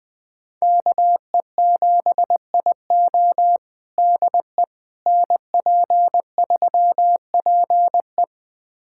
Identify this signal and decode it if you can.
Morse code, heard as a keyed tone, about 20 wpm, 700 Hz. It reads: KE7IO DE NP3PE
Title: USA Callsign DE Callsign - 5 Characters - 20wpm